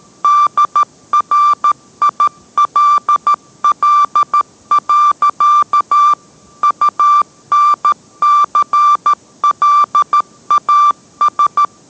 Morse code, heard as a keyed tone, about 13 wpm, 1180 Hz. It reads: DRILL. UNCLAS